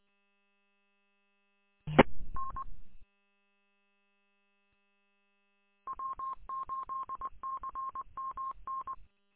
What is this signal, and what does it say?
Morse code, heard as a keyed tone, about 20 wpm, 1070 Hz. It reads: E N W8CMN
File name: w8cmn- -combined